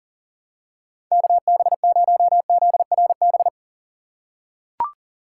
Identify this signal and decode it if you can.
Morse code, heard as a keyed tone, about 40 wpm, 700 Hz. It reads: KB0ZRB
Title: USA Call Signs - 6 Characters - 4x Character Spacing - 40wpm